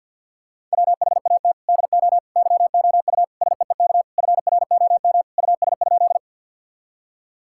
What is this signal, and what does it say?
Morse code, heard as a keyed tone, about 50 wpm, 700 Hz. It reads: WHAT DO YOU SEEK FROM US?
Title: Sentences from Top 1600 Words 50wpm